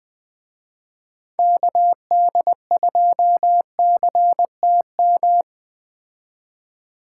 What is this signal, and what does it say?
Morse code, heard as a keyed tone, about 19 wpm, 700 Hz. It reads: KD2CTM